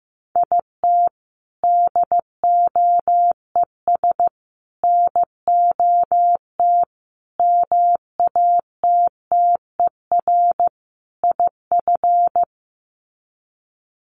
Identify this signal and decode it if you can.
Morse code, heard as a keyed tone, about 15 wpm, 700 Hz. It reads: IT DOES NOT MATTER IF
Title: Sets of 5 Words Encore 15wpm